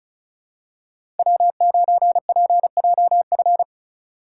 Morse code, heard as a keyed tone, about 36 wpm, 700 Hz. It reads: W9PJF
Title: USA Call Signs - 5 Characters - 2x Character Spacing - 35wpm